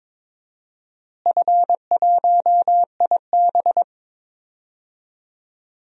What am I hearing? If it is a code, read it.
Morse code, heard as a keyed tone, about 22 wpm, 700 Hz. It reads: F1IB
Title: DXCC-100 Call Signs - 4 Characters - Repeated 3x - 22wpm